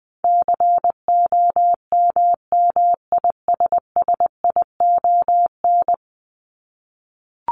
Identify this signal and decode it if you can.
Morse code, heard as a keyed tone, about 20 wpm, 700 Hz. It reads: COMMISSION
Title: Top 1251-1500 Words 20wpm